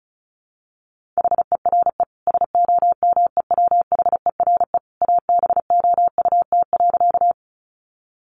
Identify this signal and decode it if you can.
Morse code, heard as a keyed tone, about 34 wpm, 700 Hz. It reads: HERE SOMEWHERE ABOUT.
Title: Sentences from Top 1400 Words 35wpm